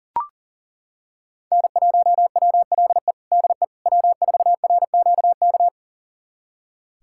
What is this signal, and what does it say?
Morse code, heard as a keyed tone, about 40 wpm, 700 Hz. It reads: N1WLE DE W4RQK
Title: USA Callsign DE Callsign - 5 Characters - 40wpm